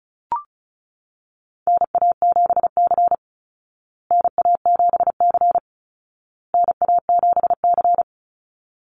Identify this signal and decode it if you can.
Morse code, heard as a keyed tone, about 34 wpm, 700 Hz. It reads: NA7C NA7C NA7C
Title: Call Signs Repeated 2x - 35wpm